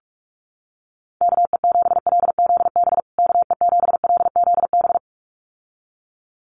Morse code, heard as a keyed tone, about 44 wpm, 700 Hz. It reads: KE7LZB KE7LZB
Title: USA Call Signs Repeated 2x 45wpm